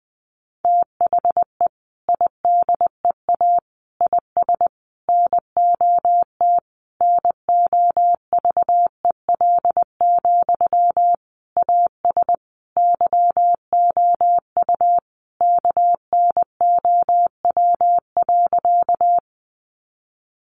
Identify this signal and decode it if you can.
Morse code, heard as a keyed tone, about 20 wpm, 700 Hz. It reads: THE IDEA IS NOT NOVEL, AS YOU KNOW.
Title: Sentences from Top 1300 Words 20wpm